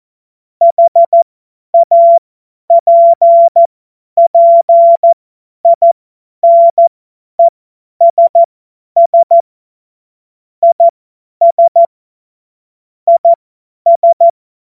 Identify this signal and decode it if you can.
Morse code, heard as a keyed tone, about 14 wpm, spaced as for 10 wpm, 680 Hz. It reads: HAPPINESS IS IS
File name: Morsekurs / Witze / Witz 14. Dezember 2021, 50 F70